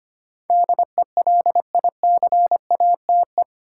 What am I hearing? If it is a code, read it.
Morse code, heard as a keyed tone, about 25 wpm, 700 Hz. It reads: DELICATE
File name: 8 Letter Words - First 3 Letters Given - 25wpm